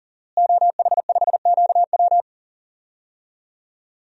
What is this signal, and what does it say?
Morse code, heard as a keyed tone, about 37 wpm, 700 Hz. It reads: OH5QW
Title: DXCC-100 Call Signs - 5 Characters - Repeated 2x - 40wpm